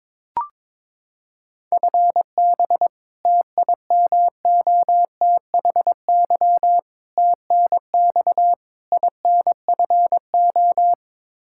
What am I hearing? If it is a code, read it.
Morse code, heard as a keyed tone, about 22 wpm, 700 Hz. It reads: FB TIMOTHY TNX INFO
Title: K1USN SST QSO Practice 22wpm